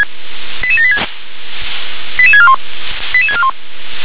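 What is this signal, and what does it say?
Five-tone selective call (ZVEI), heard as quick runs of five tones, tones of about 80 ms. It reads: 88786 88641 88641
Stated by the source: Received on SDR, ZVEI five-tone call 88747.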